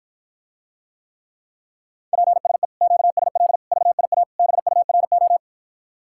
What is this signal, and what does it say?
Morse code, heard as a keyed tone, about 53 wpm, 700 Hz. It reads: PSE QSL VIA BURO